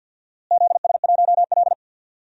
Morse code, heard as a keyed tone, about 51 wpm, 700 Hz. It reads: ZS1L
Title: DXCC-100 Call Signs - 4 Characters - Repeated 3x - 50wpm